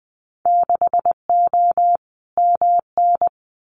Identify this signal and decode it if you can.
Morse code, heard as a keyed tone, about 20 wpm, 700 Hz. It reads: 6O MN
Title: Winter Field Day Exchanges Encore 20wpm